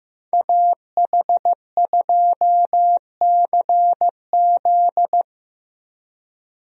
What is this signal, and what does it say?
Morse code, heard as a keyed tone, about 16 wpm, 700 Hz. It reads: AH2CZ
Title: USA Call Signs - 5 Characters - Repeated 3x - 15wpm